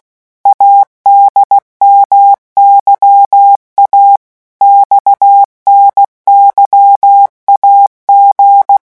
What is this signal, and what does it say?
Morse code, heard as a keyed tone, about 16 wpm, 800 Hz. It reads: ADMYA XNYAG
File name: morse mp3 / revision3-vitesse 15 mots